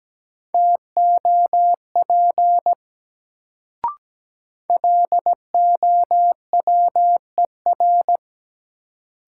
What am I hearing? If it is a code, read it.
Morse code, heard as a keyed tone, about 17 wpm, 700 Hz. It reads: TOP LOWER
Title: Sentences from Top 700 Words - New Word Drill - 17wpm